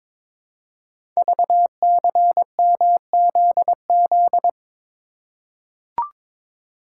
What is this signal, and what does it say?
Morse code, heard as a keyed tone, about 22 wpm, 700 Hz. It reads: VCMZZ
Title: Code Groups - 5 Letters - 22wpm